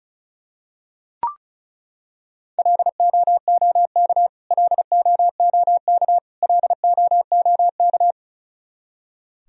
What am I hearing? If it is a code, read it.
Morse code, heard as a keyed tone, about 36 wpm, 700 Hz. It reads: LOOK LOOK LOOK
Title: Top 100 Words Repeated 35wpm